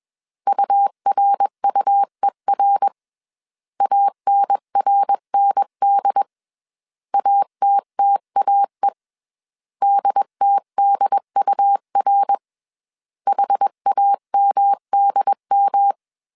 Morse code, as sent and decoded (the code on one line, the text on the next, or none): ..- .-. ..- . .-. / .- -. .-. -. -.. / .- - - .- . / -.. - -.. ..- .-. / .... .- -- -.. --
URUER ANRND ATTAE DTDUR HAMDM